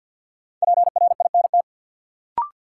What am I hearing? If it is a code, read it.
Morse code, heard as a keyed tone, about 53 wpm, 700 Hz. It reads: PRINT
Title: Sentences from Top 1400 Words - New Word Drill - 50wpm